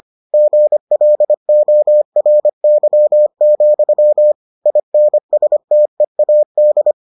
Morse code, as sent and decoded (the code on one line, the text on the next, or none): --. .-.. --- .-. -.-- --..-- / .. -. ... - . .- -..
GLORY, INSTEAD